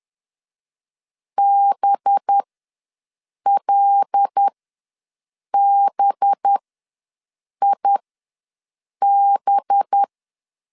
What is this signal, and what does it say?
Morse code, heard as a keyed tone, about 10 wpm, spaced as for 6 wpm, 800 Hz. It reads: BLBIB
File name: LEZIONI / 5 LBVC / 5 LB ws 10wpm 3616